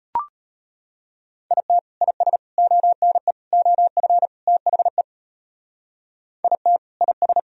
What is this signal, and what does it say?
Morse code, heard as a keyed tone, about 38 wpm, 700 Hz. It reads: IT IS ONE OF THE IT IS